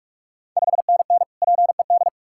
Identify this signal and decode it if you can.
Morse code, heard as a keyed tone, about 44 wpm, 690 Hz. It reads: 5NN PED
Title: Ontario QSO Party Exchange 45wpm